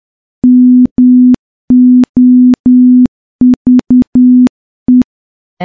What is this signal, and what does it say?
Morse code, heard as a keyed tone, about 10 wpm, 250 Hz. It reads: MOVE
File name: morse-training-10WPM